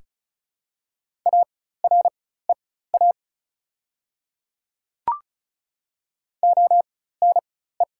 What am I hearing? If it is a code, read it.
Morse code, heard as a keyed tone, about 36 wpm, spaced as for 16 wpm, 690 Hz. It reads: AREA ONE
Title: 4x Character Spacing - Top 500 Words - 35wpm